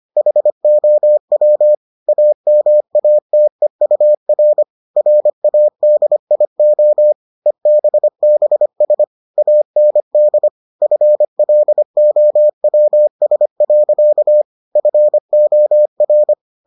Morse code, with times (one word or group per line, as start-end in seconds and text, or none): 0.2-1.7 HOW
2.1-4.6 AMATEUR
5.0-7.1 RADIO
7.5-9.0 EBBS
9.4-10.5 AND
10.8-14.4 FLOWS.
14.8-16.3 FOR